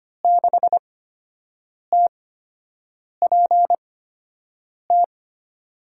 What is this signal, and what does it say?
Morse code, heard as a keyed tone, about 25 wpm, spaced as for 7 wpm, 700 Hz. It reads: 6TPT
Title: USA Call Signs - 5 Characters - 8x Character Spacing - 25wpm